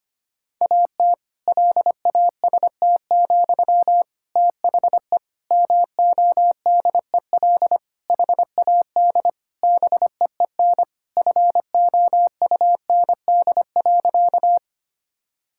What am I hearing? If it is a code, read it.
Morse code, heard as a keyed tone, about 25 wpm, 700 Hz. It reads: AT LAST, THE MODEL HAD BEEN FOUND.
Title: Sentences from Top 1000 Words - Review Part 5 - 25wpm